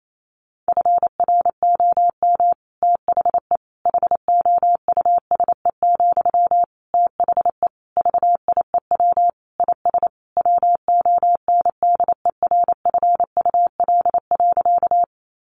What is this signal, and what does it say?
Morse code, heard as a keyed tone, about 28 wpm, 700 Hz. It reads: FROM THE HOUSE, THE VIEW IS WONDERFUL.
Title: Sentences from Top 1100 Words 28wpm